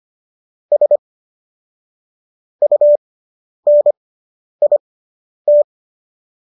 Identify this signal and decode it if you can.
Morse code, heard as a keyed tone, about 25 wpm, spaced as for 10 wpm, 590 Hz. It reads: S UNIT